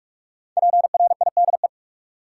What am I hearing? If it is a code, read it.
Morse code, heard as a keyed tone, about 45 wpm, 700 Hz. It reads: PRIDE
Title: Five Letter Words - 45wpm